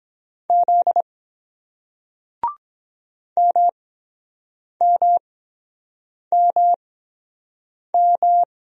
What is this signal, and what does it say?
Morse code, heard as a keyed tone, about 22 wpm, spaced as for 7 wpm, 700 Hz. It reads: Z MMMM